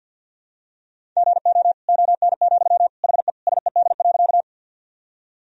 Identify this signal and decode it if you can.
Morse code, heard as a keyed tone, about 50 wpm, 700 Hz. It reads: GO ON, HE SED.
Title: Sentences from Top 100 Words and Abbreviations 50wpm